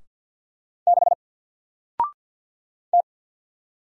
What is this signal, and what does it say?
Morse code, heard as a keyed tone, about 48 wpm, 700 Hz. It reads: 6 T
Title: Single Letter-Number Alt 50wpm